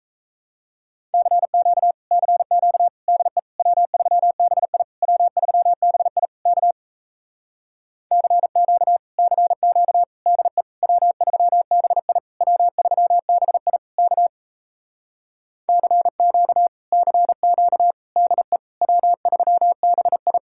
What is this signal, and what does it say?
Morse code, heard as a keyed tone, about 37 wpm, 700 Hz. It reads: CQ CQ DE W3BI W3BI K CQ CQ DE W3BI W3BI K CQ CQ DE W3BI